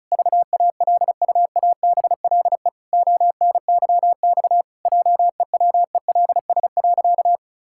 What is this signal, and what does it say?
Morse code, heard as a keyed tone, about 36 wpm, 700 Hz. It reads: VALUABLE ONYX JEWELS.
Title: Pangram Challenge v2 - 35wpm